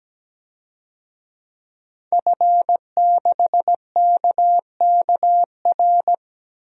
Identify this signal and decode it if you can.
Morse code, heard as a keyed tone, about 17 wpm, 700 Hz. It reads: F6KKR
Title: DXCC-100 Call Signs - 5 Characters - 17wpm